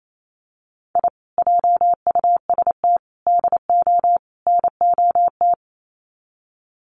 Morse code, heard as a keyed tone, about 28 wpm, 700 Hz. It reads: I JUST DO NOT
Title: Sets of 4 Words 28wpm